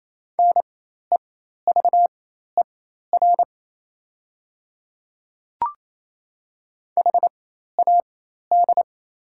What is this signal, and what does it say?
Morse code, heard as a keyed tone, about 28 wpm, spaced as for 13 wpm, 700 Hz. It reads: NEVER HAD
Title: 4x Character Spacing - Top 300 Words - 28wpm